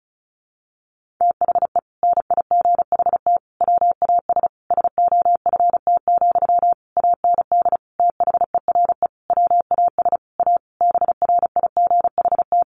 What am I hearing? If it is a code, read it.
Morse code, heard as a keyed tone, about 34 wpm, 700 Hz. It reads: THE NIGHT WAS SOFT, AND THERE WAS A BRIGHT
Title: Sentences from Top 1500 Words 35wpm